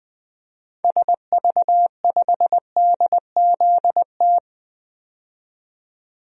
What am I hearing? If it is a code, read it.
Morse code, heard as a keyed tone, about 20 wpm, 700 Hz. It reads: SV5DZT
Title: DXCC-100 Call Signs - 6 Characters - 20wpm